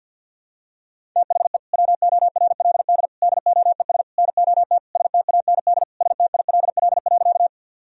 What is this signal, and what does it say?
Morse code, heard as a keyed tone, about 50 wpm, 700 Hz. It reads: THE WORLD DOES NOT STAND STILL.
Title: Sentences from Top 1000 Words - Review Part 1 - 50wpm